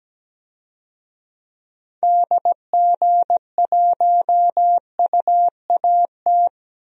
Morse code, heard as a keyed tone, about 17 wpm, 700 Hz. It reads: DG1UAT
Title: DXCC-100 Call Signs - 6 Characters - 17wpm